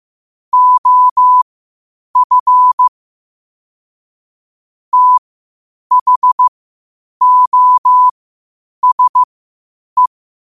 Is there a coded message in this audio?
Yes: Morse code from 0.5 to 10.1 s, heard as a keyed tone, about 15 wpm, spaced as for 8 wpm, 1000 Hz. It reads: OF THOSE